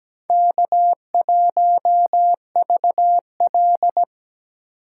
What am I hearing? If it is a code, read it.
Morse code, heard as a keyed tone, about 20 wpm, 700 Hz. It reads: K1VL
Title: Speed Racing - USA 1x2 Call Signs 17wpm